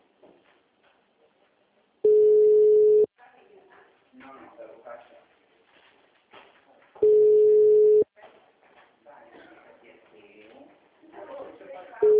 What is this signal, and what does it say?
Ringback tone (Europe, 425 Hz), with speech.
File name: out- -2003- - - .1264